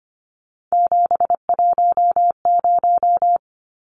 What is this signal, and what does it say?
Morse code, heard as a keyed tone, about 25 wpm, 700 Hz. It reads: 710
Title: Numbers - 3-Digit 25wpm